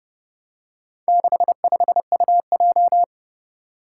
Morse code, heard as a keyed tone, about 28 wpm, 700 Hz. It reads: 65UJ
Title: Four Challenging Letter-Number Combinations 30wpm